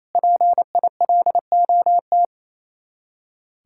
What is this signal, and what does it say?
Morse code, heard as a keyed tone, about 34 wpm, 700 Hz. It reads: PILOT
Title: Speed-Racing Common QSO Phrases Encore - 28wpm